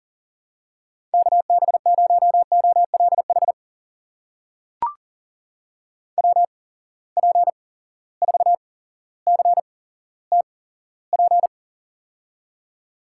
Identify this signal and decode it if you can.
Morse code, heard as a keyed tone, about 40 wpm, spaced as for 17 wpm, 700 Hz. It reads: KB0OLH WP4CTP
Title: USA Call Signs - 6 Characters - 8x Character Spacing - 40wpm